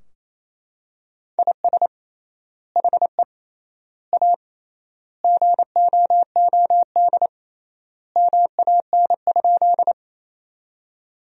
Morse code, heard as a keyed tone, about 27 wpm, 700 Hz. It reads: IS HE A GOOD MAN?
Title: 3x Word-Spacing - Sentences from Top 200 Words - 28wpm